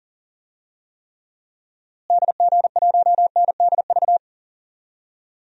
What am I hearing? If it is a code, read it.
Morse code, heard as a keyed tone, about 40 wpm, 700 Hz. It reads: DG1NDV
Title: DXCC-100 Call Signs - 6 Characters - Repeated 3x - 40wpm